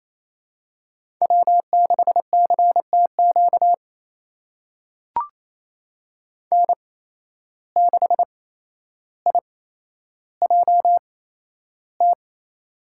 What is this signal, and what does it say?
Morse code, heard as a keyed tone, about 28 wpm, spaced as for 12 wpm, 700 Hz. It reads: W6CTQ N6IJT
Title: USA Call Signs - 5 Characters - 8x Character Spacing - 28wpm